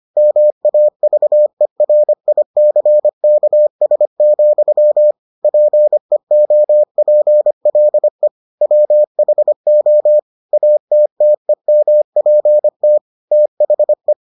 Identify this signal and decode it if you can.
Morse code, heard as a keyed tone, about 25 wpm, 600 Hz. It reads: MAVERICKS, PEOPLE WHO ATTEMPT THE